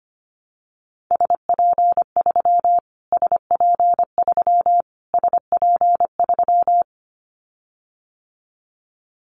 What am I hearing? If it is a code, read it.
Morse code, heard as a keyed tone, about 25 wpm, 700 Hz. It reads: SP3 SP3 SP3
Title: Callsign Prefixes for Top 25 DXCC - Repeated 3x - 25wpm